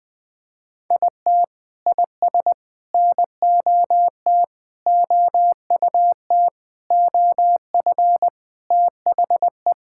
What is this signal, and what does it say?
Morse code, heard as a keyed tone, about 20 wpm, 700 Hz. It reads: IT IS NOT OUT OF THE
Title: Sentences with 9-Word Phrases - Spoken Beginning - 20wpm